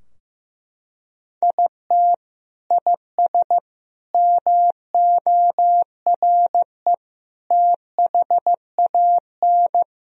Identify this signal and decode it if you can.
Morse code, heard as a keyed tone, about 15 wpm, 700 Hz. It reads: IT IS MORE THAN
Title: Sentences from Top 1000 Words - Review Part 4 - 15wpm